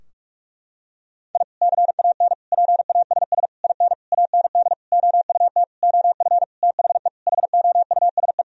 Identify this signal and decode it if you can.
Morse code, heard as a keyed tone, about 46 wpm, 700 Hz. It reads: I CAN PASS IN AND OUT OF THE HOUSE
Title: Sentences from Top 1000 Words - Review Part 4 - 45wpm